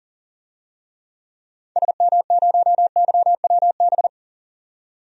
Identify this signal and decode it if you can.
Morse code, heard as a keyed tone, about 39 wpm, 700 Hz. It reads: SM0YWB
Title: DXCC-100 Call Signs - 6 Characters - 40wpm